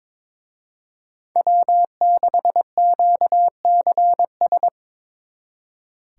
Morse code, heard as a keyed tone, about 26 wpm, 700 Hz. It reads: W6QCS
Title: Speed Racing - USA 1x3 Call Signs 22wpm